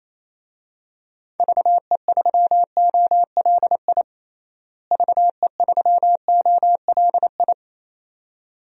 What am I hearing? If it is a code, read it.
Morse code, heard as a keyed tone, about 28 wpm, 700 Hz. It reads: VE3OLI VE3OLI
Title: DXCC-100 Call Signs - 6 Characters - Repeated 2x - 28wpm